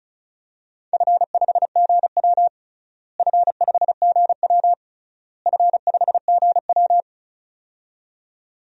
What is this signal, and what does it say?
Morse code, heard as a keyed tone, about 34 wpm, 700 Hz. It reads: F5GW F5GW F5GW
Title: DXCC-100 Call Signs - 4 Characters - Repeated 3x - 35wpm